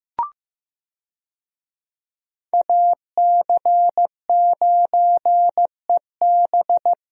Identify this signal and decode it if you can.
Morse code, heard as a keyed tone, about 15 wpm, 700 Hz. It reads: AC9EB
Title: USA Call Signs - 5 Characters - Repeated 3x - 15wpm